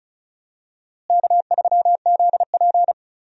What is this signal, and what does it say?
Morse code, heard as a keyed tone, about 36 wpm, 700 Hz. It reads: K3ZP
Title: USA Call Signs - 4 Characters - Repeated 2x - 35wpm